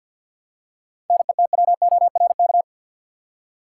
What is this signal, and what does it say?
Morse code, heard as a keyed tone, about 53 wpm, 700 Hz. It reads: NETWORK
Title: Seven Letter Words - 50wpm